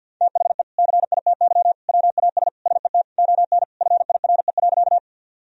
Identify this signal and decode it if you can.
Morse code, heard as a keyed tone, about 47 wpm, 700 Hz. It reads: THE CITY WAS SET ON FIRE.
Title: Sentences from Top 1000 Words - Review Part 1 - 50wpm